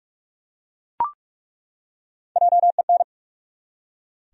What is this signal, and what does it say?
Morse code, heard as a keyed tone, about 42 wpm, 700 Hz. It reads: JEN
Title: USA Common Names - Max 3 Letters - 45wpm